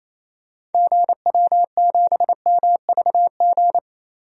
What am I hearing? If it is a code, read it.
Morse code, heard as a keyed tone, about 28 wpm, 700 Hz. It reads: GW7MVG
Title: DXCC-100 Call Signs - 6 Characters - 28wpm